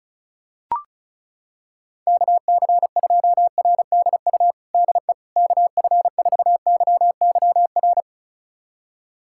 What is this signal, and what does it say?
Morse code, heard as a keyed tone, about 34 wpm, 690 Hz. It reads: KC2RDU DE KF4YYR
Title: USA Callsign DE Callsign 35wpm